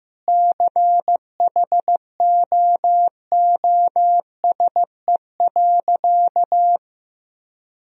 Morse code, heard as a keyed tone, about 15 wpm, 700 Hz. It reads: CHOOSE.
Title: Sentences from Top 1000 Words - Review Part 4 - 15wpm